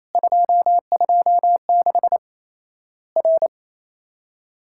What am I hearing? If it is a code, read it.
Morse code, heard as a keyed tone, about 28 wpm, 700 Hz. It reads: 226 R
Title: Worked All Europe DX Contest - QTC Exchange - 28wpm